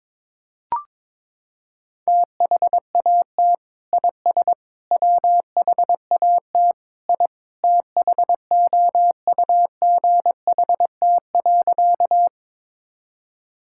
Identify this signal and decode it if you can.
Morse code, heard as a keyed tone, about 22 wpm, 700 Hz. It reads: THAT IS WHAT I THOUGHT.
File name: practice / 22wpm / Sentences with 2-Word Phrases - 22wpm